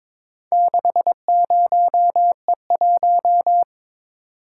Morse code, heard as a keyed tone, about 22 wpm, 690 Hz. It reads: 60E1
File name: practice / 22wpm / Four Letter-Number Combinations - 22wpm